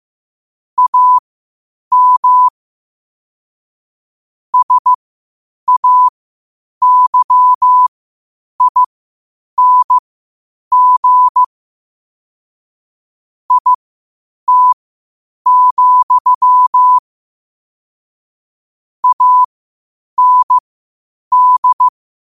Morse code, heard as a keyed tone, about 15 wpm, spaced as for 8 wpm, 1000 Hz. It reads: AM SAYING IT, AND
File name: qotdfiles / 8wpm